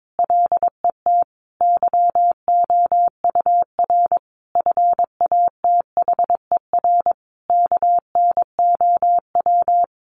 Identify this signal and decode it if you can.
Morse code, heard as a keyed tone, about 22 wpm, 700 Hz. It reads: LET YOUR FATHER KNOW
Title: Sentences from Top 1500 Words 22wpm